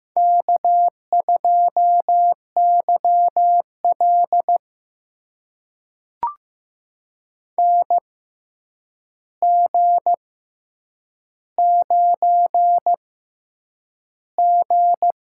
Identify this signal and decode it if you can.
Morse code, heard as a keyed tone, about 15 wpm, spaced as for 8 wpm, 700 Hz. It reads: K2YL NG9G
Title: USA Call Signs - 4 Characters - 6x Character Spacing - 15wpm